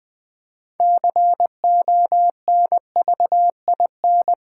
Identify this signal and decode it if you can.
Morse code, heard as a keyed tone, about 20 wpm, 700 Hz. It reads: CONVIN
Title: Build A Word - 3 Letters At A Time - 9-Letter Words - 20wpm